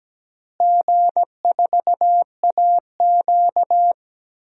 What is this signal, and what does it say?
Morse code, heard as a keyed tone, about 17 wpm, 700 Hz. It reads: G4AQ
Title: DXCC-100 Call Signs - 4 Characters - Repeated 3x - 17wpm